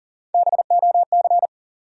Morse code, heard as a keyed tone, about 40 wpm, 700 Hz. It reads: BOC
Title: USA Call Sign Postfixes - 3 Letters 40wpm